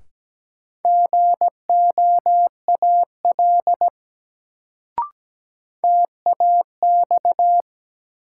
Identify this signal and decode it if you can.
Morse code, heard as a keyed tone, about 17 wpm, 700 Hz. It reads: GOAL TAX
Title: Sentences from Top 1000 Words - Review Part 5 - New Word Drill - 17wpm